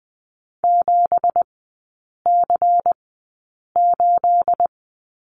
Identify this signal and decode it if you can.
Morse code, heard as a keyed tone, about 20 wpm, spaced as for 8 wpm, 700 Hz. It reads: 7C8
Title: 218 Lesson 35e - Number 0 - Three Characters 20wpm